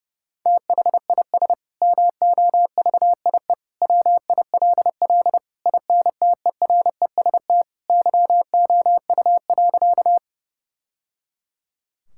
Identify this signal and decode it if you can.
Morse code, heard as a keyed tone, about 30 wpm, 700 Hz. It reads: THIS MOVIE WILL INTEREST YOU.
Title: Sentences from Top 1000 Words - Review Part 5 - 30wpm